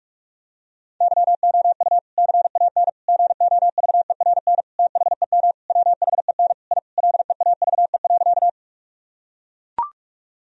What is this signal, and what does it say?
Morse code, heard as a keyed tone, about 41 wpm, 700 Hz. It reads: YOU CAN GOVERN THEM WHEN I LEAVE.
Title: Sentences from Top 1000 Words - Review Part 2 - 45wpm